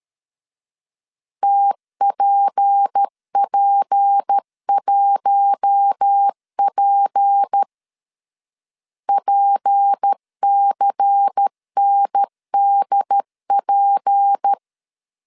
Morse code, heard as a keyed tone, about 13 wpm, 800 Hz. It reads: TPP1P PCNDP
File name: LEZIONI / 6 PKX0 / 6 P 12wpm 2033